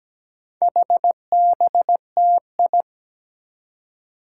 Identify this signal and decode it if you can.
Morse code, heard as a keyed tone, about 17 wpm, 700 Hz. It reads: HBTI
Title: Four Challenging Letter-Number Combinations 17wpm